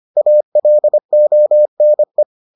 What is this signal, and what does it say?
Morse code, heard as a keyed tone, about 25 wpm, 600 Hz. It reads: ALONE